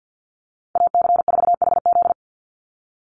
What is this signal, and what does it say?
Morse code, heard as a keyed tone, about 49 wpm, 700 Hz. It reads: AC4HZ
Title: USA Call Signs 50wpm